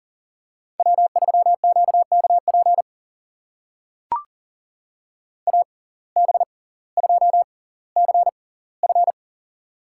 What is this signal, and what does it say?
Morse code, heard as a keyed tone, about 39 wpm, spaced as for 20 wpm, 700 Hz. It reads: W3QKP AB2CF
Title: USA Call Signs - 5 Characters - 6x Character Spacing - 40wpm